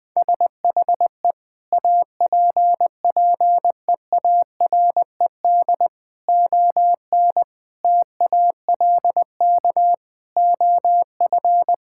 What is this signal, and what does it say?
Morse code, heard as a keyed tone, about 21 wpm, 700 Hz. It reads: SHE APPEARED ON TALK OF
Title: Sentences with 5-Word Phrases - 20wpm